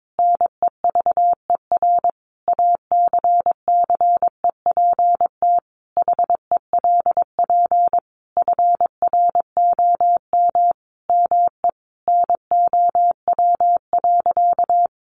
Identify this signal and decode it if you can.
Morse code, heard as a keyed tone, about 22 wpm, 700 Hz. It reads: NEVER ACCEPT HELP FROM ME NOW.
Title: Sentences from Top 1300 Words 22wpm